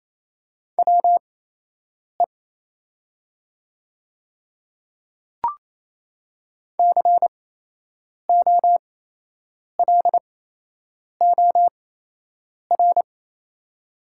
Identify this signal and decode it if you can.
Morse code, heard as a keyed tone, about 28 wpm, spaced as for 8 wpm, 700 Hz. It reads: WE COLOR
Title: 8x Character Spacing - Top 300 Words - 28wpm